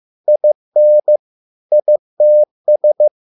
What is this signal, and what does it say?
Morse code, heard as a keyed tone, about 15 wpm, 600 Hz. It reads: IN ITS